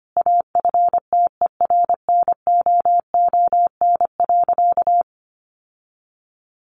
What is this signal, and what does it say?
Morse code, heard as a keyed tone, about 25 wpm, 700 Hz. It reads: AFTERNOON.
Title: Sentences from Top 1500 Words 25wpm